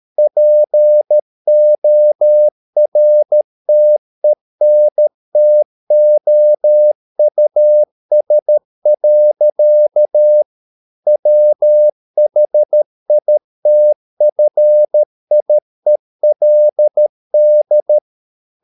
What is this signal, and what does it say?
Morse code, heard as a keyed tone, about 13 wpm, 600 Hz. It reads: PORTENTOUS. WHITFIELD